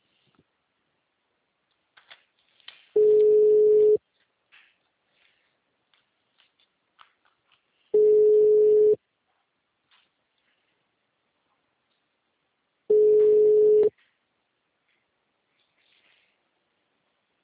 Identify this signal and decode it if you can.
Ringback tone (Europe, 425 Hz).
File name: out- -2004- - - .14